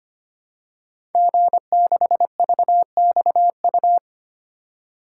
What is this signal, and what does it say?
Morse code, heard as a keyed tone, about 25 wpm, 700 Hz. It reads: G6VXU
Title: Code Groups - 5 Letter:Numbers - 25wpm